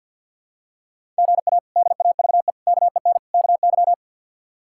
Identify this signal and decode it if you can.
Morse code, heard as a keyed tone, about 47 wpm, 700 Hz. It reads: GA DAVE KEN KY
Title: K1USN SST QSO Practice - 50wpm